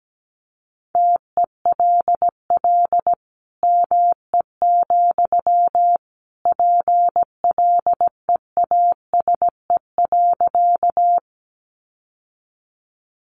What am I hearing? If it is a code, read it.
Morse code, heard as a keyed tone, about 17 wpm, 700 Hz. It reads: TELL ME, PLEASE.
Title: Sentences from Top 1500 Words 17wpm